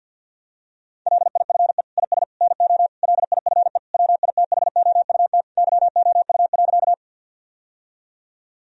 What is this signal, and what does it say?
Morse code, heard as a keyed tone, about 50 wpm, 700 Hz. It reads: LIFE IS NO LIFE WITHOUT YOU.
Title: Sentences from Top 1000 Words - Review Part 3 - 50wpm